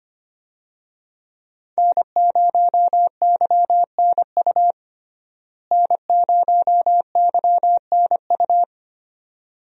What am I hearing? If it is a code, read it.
Morse code, heard as a keyed tone, about 25 wpm, 700 Hz. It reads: N0YNU N0YNU
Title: USA Call Signs - 5 Characters - Repeated 2x - 25wpm